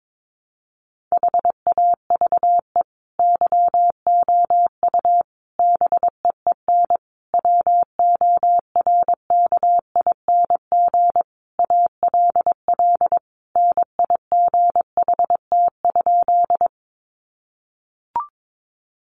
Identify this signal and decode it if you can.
Morse code, heard as a keyed tone, about 22 wpm, 700 Hz. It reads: HAVE YOU BEEN WORKING ALL NIGHT?
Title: Sentences from Top 700 Words 22wpm